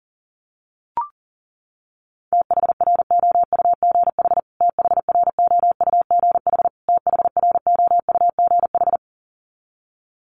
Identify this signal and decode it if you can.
Morse code, heard as a keyed tone, about 40 wpm, 700 Hz. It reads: THROUGH THROUGH THROUGH
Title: Top 500 Words Repeated 40wpm